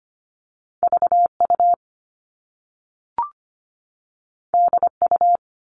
Morse code, heard as a keyed tone, about 25 wpm, 700 Hz. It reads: VU DU
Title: All Cut Numbers - 2-Digit 25wpm